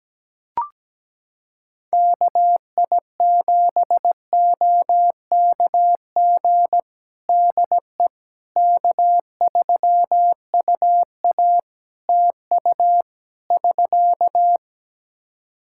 Morse code, heard as a keyed tone, about 17 wpm, 700 Hz. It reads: KI7OKG DE K3UA TU <SK>
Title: Practice QSO - Exchange 4 - Ending - 17wpm